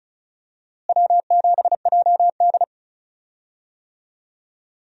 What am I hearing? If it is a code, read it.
Morse code, heard as a keyed tone, about 34 wpm, 700 Hz. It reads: W7JD
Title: USA Call Signs - 4 Characters - Repeated 3x - 35wpm